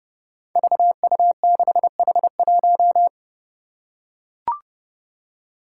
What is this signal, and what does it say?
Morse code, heard as a keyed tone, about 30 wpm, 700 Hz. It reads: VU6H1
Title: Five Challenging Letter-Number Combinations 30wpm